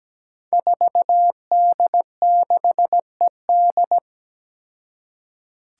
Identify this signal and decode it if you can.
Morse code, heard as a keyed tone, about 17 wpm, 700 Hz. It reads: 4D6ED
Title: Five Challenging Letter-Number Combinations 17wpm